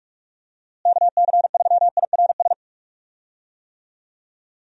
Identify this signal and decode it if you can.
Morse code, heard as a keyed tone, about 42 wpm, 700 Hz. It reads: KC3IRS
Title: USA Call Signs - 6 Characters - Repeated 3x - 45wpm